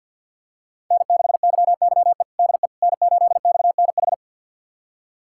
Morse code, heard as a keyed tone, about 50 wpm, 700 Hz. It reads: N6YYE DE N8XNH
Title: USA Callsign DE Callsign - 5 Characters - 50wpm